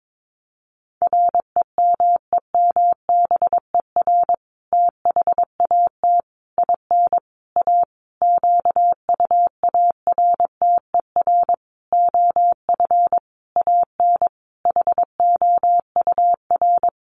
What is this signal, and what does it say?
Morse code, heard as a keyed tone, about 22 wpm, 700 Hz. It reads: REMEMBER THAT IN A QUARTER OF AN HOUR
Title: Sentences from Top 1300 Words 22wpm